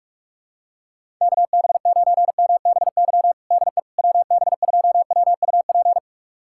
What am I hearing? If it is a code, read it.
Morse code, heard as a keyed tone, about 46 wpm, 690 Hz. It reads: KB9MBY DE WB2WUP
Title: USA Callsign DE Callsign - 6 Characters - 45wpm